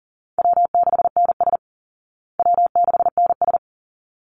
Morse code, heard as a keyed tone, about 40 wpm, 700 Hz. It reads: W6NS W6NS
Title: Easy Call Signs Repeated 2x - 40wpm